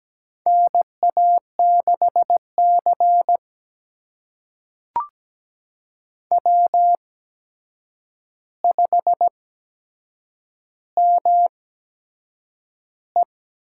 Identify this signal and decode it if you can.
Morse code, heard as a keyed tone, about 17 wpm, spaced as for 7 wpm, 700 Hz. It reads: NA6C W5ME
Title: USA Call Signs - 4 Characters - 8x Character Spacing - 17wpm